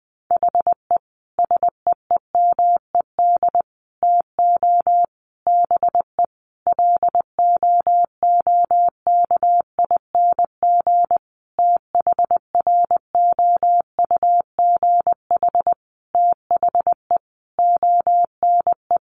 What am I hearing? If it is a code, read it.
Morse code, heard as a keyed tone, about 20 wpm, 700 Hz. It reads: HE SEEMED TO BE LOOKING THROUGH THE ONE
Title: A Sound Of Thunder 20wpm